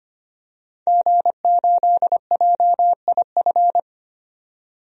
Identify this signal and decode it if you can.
Morse code, heard as a keyed tone, about 25 wpm, 700 Hz. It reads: G8JIF
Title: DXCC-100 Call Signs - 5 Characters - 25wpm